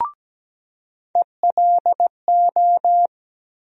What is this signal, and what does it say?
Morse code, heard as a keyed tone, about 17 wpm, 700 Hz. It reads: ELO
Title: USA Call Sign Postfixes - 3 Letters 17wpm